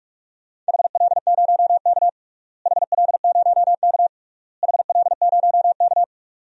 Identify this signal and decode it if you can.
Morse code, heard as a keyed tone, about 46 wpm, 700 Hz. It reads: HL0K HL0K HL0K
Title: DXCC-100 Call Signs - 4 Characters - Repeated 3x - 45wpm